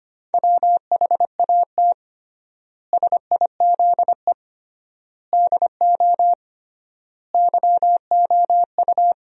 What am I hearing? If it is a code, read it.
Morse code, heard as a keyed tone, about 25 wpm, 700 Hz. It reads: WHAT SIZE DO YOU
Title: 3x Word-Spacing - Sentences from Top 400 Words - 25wpm